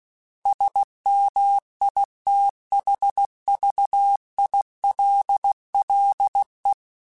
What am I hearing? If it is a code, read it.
Morse code, heard as a keyed tone, about 16 wpm, 800 Hz. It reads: SMITHVILLE
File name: fromSMs / FromSMs mp3s / Cities@15